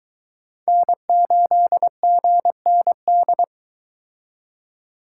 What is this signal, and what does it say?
Morse code, heard as a keyed tone, about 23 wpm, 700 Hz. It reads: N8GND